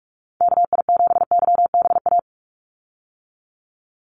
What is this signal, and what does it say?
Morse code, heard as a keyed tone, about 44 wpm, 700 Hz. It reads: KI7YBA
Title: USA Call Signs 45wpm